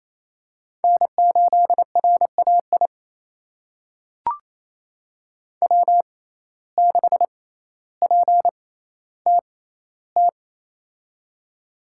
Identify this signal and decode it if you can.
Morse code, heard as a keyed tone, about 28 wpm, spaced as for 14 wpm, 700 Hz. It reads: N8RAI W6PTT
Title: USA Call Signs - 5 Characters - 6x Character Spacing - 28wpm